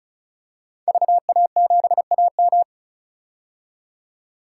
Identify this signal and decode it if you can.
Morse code, heard as a keyed tone, about 36 wpm, 700 Hz. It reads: VA7AM
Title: K1USN SST QSO Practice - 35wpm